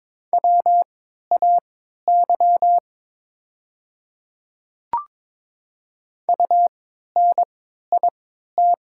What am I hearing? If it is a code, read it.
Morse code, heard as a keyed tone, about 22 wpm, spaced as for 13 wpm, 690 Hz. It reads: WAY UNIT
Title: 3x Character Spacing - Top 500 Words - 22wpm